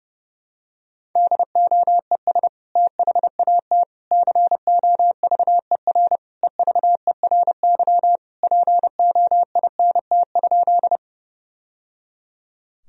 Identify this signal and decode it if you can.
Morse code, heard as a keyed tone, about 32 wpm, 700 Hz. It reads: DOES THAT COVER EVERY POINT?
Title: Sentences from Top 1000 Words - Review Part 1 - 30wpm